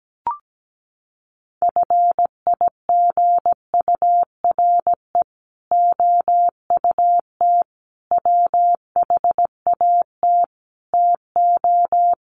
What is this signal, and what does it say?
Morse code, heard as a keyed tone, about 17 wpm, 700 Hz. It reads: FIGURE OUT WHAT TO
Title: Sets of 5 Words Encore2 17wpm